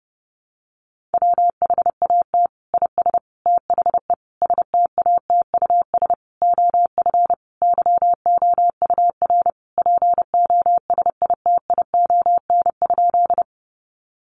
Morse code, heard as a keyed tone, about 30 wpm, 700 Hz. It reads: WHAT IS THE STATUS OF YOUR POSITION?
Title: Sentences from Top 1400 Words 30wpm